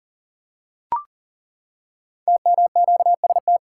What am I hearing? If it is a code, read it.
Morse code, heard as a keyed tone, about 38 wpm, 700 Hz. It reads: TMQST
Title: Code Groups - 5 Letter:Numbers - 40wpm